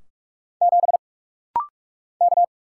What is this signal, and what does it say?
Morse code, heard as a keyed tone, about 44 wpm, 700 Hz. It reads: 7 K
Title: Single Letter-Number - Rapid-Fire 45wpm